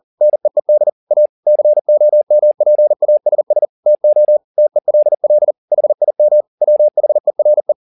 Morse code, heard as a keyed tone, about 40 wpm, 600 Hz. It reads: NEED A COMPASS TO TELL HIM WHERE
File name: the old man and the sea 2 40wpm